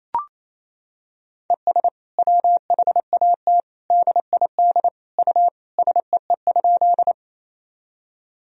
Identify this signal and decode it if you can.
Morse code, heard as a keyed tone, about 28 wpm, 700 Hz. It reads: ES WHAT DID U SEE?
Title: Sentences from Top 100 Words and Abbreviations 28wpm